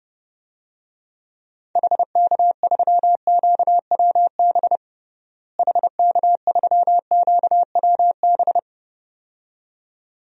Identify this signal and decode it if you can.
Morse code, heard as a keyed tone, about 30 wpm, 700 Hz. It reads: HK3QWB HK3QWB
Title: DXCC-100 Call Signs - 6 Characters - Repeated 2x - 30wpm